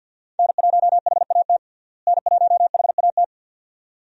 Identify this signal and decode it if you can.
Morse code, heard as a keyed tone, about 52 wpm, 700 Hz. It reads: N1HAT N1HAT
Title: USA Call Signs - 5 Characters - Repeated 2x - 50wpm